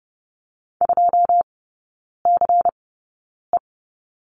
Morse code, heard as a keyed tone, about 30 wpm, spaced as for 9 wpm, 700 Hz. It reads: 2CE
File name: Lesson 18e - Letter m - Three Characters 30wpm